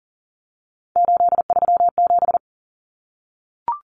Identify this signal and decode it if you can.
Morse code, heard as a keyed tone, about 40 wpm, 700 Hz. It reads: 837
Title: Numbers - 3-Digit 40wpm